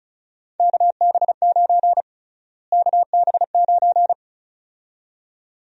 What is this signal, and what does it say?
Morse code, heard as a keyed tone, about 36 wpm, 700 Hz. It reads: KB9 KB9
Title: USA Call Sign Prefixes - Repeated 2x - 35wpm